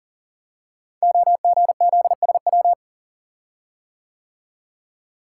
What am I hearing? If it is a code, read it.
Morse code, heard as a keyed tone, about 38 wpm, 700 Hz. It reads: OGZSW
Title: Code Groups - 5 Letter:Numbers - 40wpm